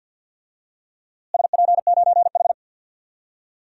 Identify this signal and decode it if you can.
Morse code, heard as a keyed tone, about 49 wpm, 700 Hz. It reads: SP9H
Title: DXCC-100 Call Signs - 4 Characters - Repeated 3x - 50wpm